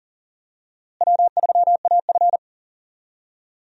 Morse code, heard as a keyed tone, about 40 wpm, 700 Hz. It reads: W3AF
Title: USA Call Signs - 4 Characters - Repeated 2x - 40wpm